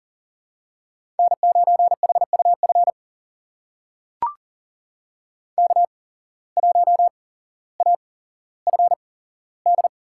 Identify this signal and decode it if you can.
Morse code, heard as a keyed tone, about 40 wpm, spaced as for 17 wpm, 700 Hz. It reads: N9HUF K1AFD
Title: USA Call Signs - 5 Characters - 8x Character Spacing - 40wpm